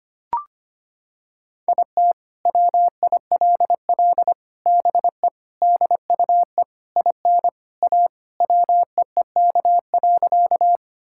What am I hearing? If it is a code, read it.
Morse code, heard as a keyed tone, about 25 wpm, 700 Hz. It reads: IT WILL BE DUE IN A WEEK.
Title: Sentences from Top 1000 Words - Review Part 6 - 25wpm